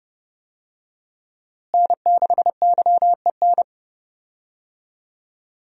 Morse code, heard as a keyed tone, about 31 wpm, 700 Hz. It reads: N6YEN
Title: USA Call Signs - 5 Characters - Repeated 3x - 30wpm